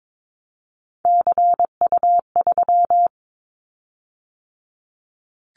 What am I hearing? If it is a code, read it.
Morse code, heard as a keyed tone, about 22 wpm, 700 Hz. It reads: CU3
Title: Callsign Prefixes for Top 100 DXCC - 22wpm